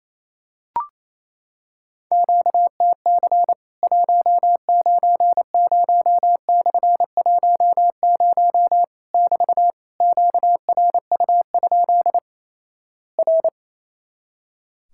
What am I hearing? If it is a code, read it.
Morse code, heard as a keyed tone, about 28 wpm, 700 Hz. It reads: QTC 190/10 <BT> QRU? R
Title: Worked All Europe DX Contest - QTC Exchange - 28wpm